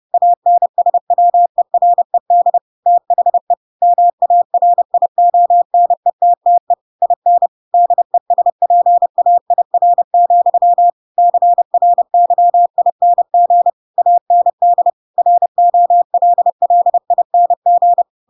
Morse code, heard as a keyed tone, about 30 wpm, 700 Hz. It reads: ANSWERED THE MARIONETTE IN DESPAIR, CRYING AND ROLLING
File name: Pinocchio / Pinocchio ch7 30 30 ik1hge